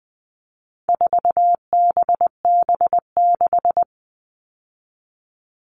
Morse code, heard as a keyed tone, about 20 wpm, 700 Hz. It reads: 4BB6
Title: All Cut Numbers - 4-Digit 20wpm